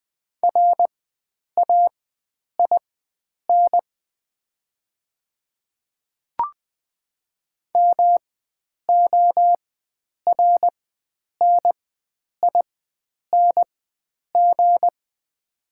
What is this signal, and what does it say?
Morse code, heard as a keyed tone, about 20 wpm, spaced as for 9 wpm, 700 Hz. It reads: RAIN MORNING
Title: 4x Character Spacing - Top 500 Words - 20wpm